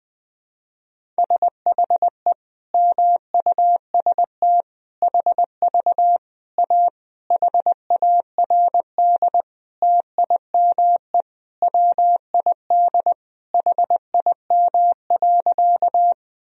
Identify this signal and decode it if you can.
Morse code, heard as a keyed tone, about 20 wpm, 700 Hz. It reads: SHE MUST HV A HARD TIME WID HIM.
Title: Sentences from Top 300 Words and Abbreviations 20wpm